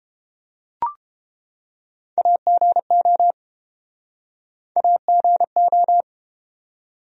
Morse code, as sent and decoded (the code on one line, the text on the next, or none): .- --. --- / .- --. ---
AGO AGO